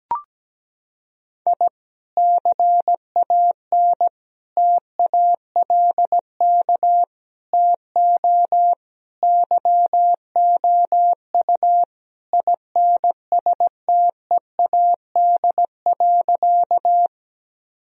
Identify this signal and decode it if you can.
Morse code, heard as a keyed tone, about 17 wpm, 700 Hz. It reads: I CAN TALK TO YOU INSTEAD.
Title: Sentences from Top 1000 Words - Review Part 4 - 17wpm